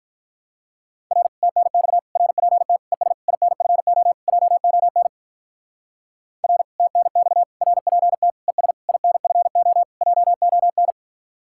Morse code, heard as a keyed tone, about 43 wpm, 700 Hz. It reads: R TNX RPT ES INFO JON R TNX RPT ES INFO JON
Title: Speed-Racing Common QSO Phrases - 35wpm